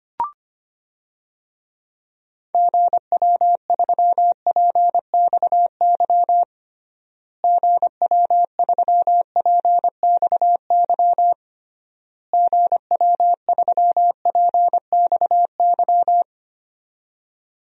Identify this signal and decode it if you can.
Morse code, heard as a keyed tone, about 25 wpm, 700 Hz. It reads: GW3PXY GW3PXY GW3PXY
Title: DXCC-100 Call Signs - 6 Characters - Repeated 3x - 25wpm